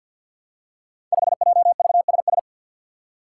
Morse code, heard as a keyed tone, about 51 wpm, 700 Hz. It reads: 5JVSS
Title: Five Challenging Letter-Number Combinations 50wpm